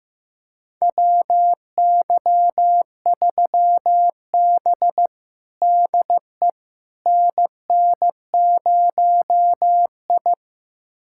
Morse code, heard as a keyed tone, about 15 wpm, 700 Hz. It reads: WY3B DE NN0I
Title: USA Callsign DE Callsign - 4 Characters - 15wpm